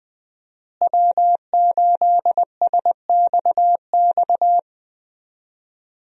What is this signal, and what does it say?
Morse code, heard as a keyed tone, about 20 wpm, 700 Hz. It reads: W8SXX
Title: USA Call Signs - 5 Characters - Repeated 3x - 20wpm